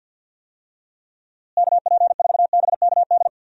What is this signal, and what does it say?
Morse code, heard as a keyed tone, about 52 wpm, 700 Hz. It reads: KP4BKD
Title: DXCC-100 Call Signs - 6 Characters - Repeated 3x - 50wpm